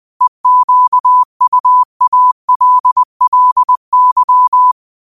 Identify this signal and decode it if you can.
Morse code, heard as a keyed tone, about 20 wpm, 1000 Hz. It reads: EQUALLY
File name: qotdfiles / 20wpm